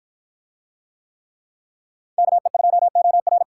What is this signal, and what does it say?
Morse code, heard as a keyed tone, about 53 wpm, 700 Hz. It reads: KE2OR